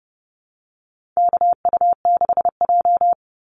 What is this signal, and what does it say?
Morse code, heard as a keyed tone, about 30 wpm, 700 Hz. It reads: KU6J
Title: NA SOTA Activator Callsigns 30wpm